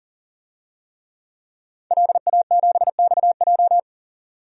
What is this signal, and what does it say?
Morse code, heard as a keyed tone, about 40 wpm, 700 Hz. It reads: LA7XJ
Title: DXCC-100 Call Signs - 5 Characters - 40wpm